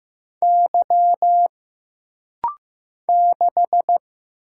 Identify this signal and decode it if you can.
Morse code, heard as a keyed tone, about 15 wpm, 700 Hz. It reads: Y 6
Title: Single Letter-Number Alt - Rapid-Fire 15wpm